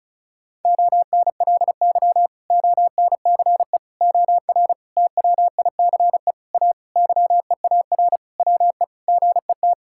Morse code, heard as a keyed tone, about 36 wpm, 700 Hz. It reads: ONLY ONCE OR TWICE A YEAR WE GET
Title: Sentences from Top 1600 Words 35wpm